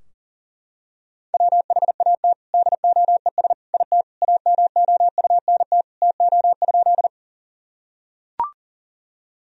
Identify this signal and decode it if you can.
Morse code, heard as a keyed tone, about 40 wpm, 700 Hz. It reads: WHAT DOES IT AMOUNT TO?
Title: Sentences from Top 1000 Words - Review Part 4 - 40wpm